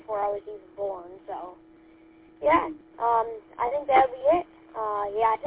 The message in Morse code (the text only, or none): none